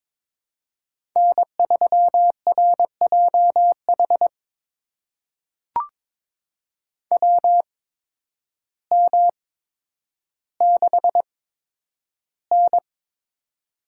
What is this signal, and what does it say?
Morse code, heard as a keyed tone, about 22 wpm, spaced as for 10 wpm, 700 Hz. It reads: N3RJH WM6N
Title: USA Call Signs - 8x Character Spacing - 22wpm